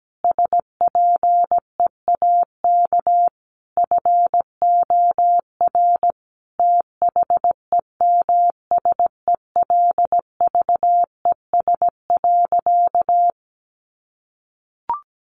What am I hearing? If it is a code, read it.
Morse code, heard as a keyed tone, about 17 wpm, 700 Hz. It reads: SPEAK FOR THEMSELVES.
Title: Sentences from Top 1400 Words 17wpm